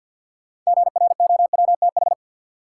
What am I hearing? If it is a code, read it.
Morse code, heard as a keyed tone, about 53 wpm, 700 Hz. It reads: GROWTH
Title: Sentences from Top 1300 Words - New Word Drill Alt - 50wpm